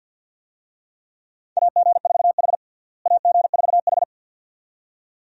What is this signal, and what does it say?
Morse code, heard as a keyed tone, about 50 wpm, 700 Hz. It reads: AG4H AG4H
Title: USA Call Signs - 4 Characters - Repeated 2x - 50wpm